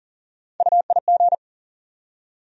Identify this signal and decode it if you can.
Morse code, heard as a keyed tone, about 40 wpm, 700 Hz. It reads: UIG
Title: USA Call Sign Postfixes - 3 Letters 40wpm